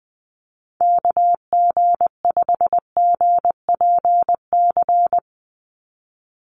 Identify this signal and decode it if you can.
Morse code, heard as a keyed tone, about 20 wpm, 700 Hz. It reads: KG5GPC
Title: USA Call Signs 20wpm